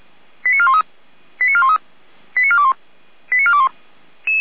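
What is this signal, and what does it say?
Five-tone selective call (ZVEI), heard as quick runs of five tones, tones of about 70 ms. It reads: 89413 89413 89421 89421
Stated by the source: ZVEI five-tone selective call 89711 received on SDR.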